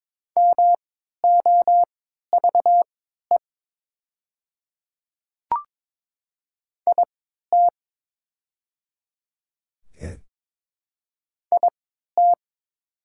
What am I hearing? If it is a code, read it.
Morse code, heard as a keyed tone, about 22 wpm, spaced as for 12 wpm, 700 Hz. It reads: MOVE IT IT
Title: 3x Character Spacing - Top 200 Words - 22wpm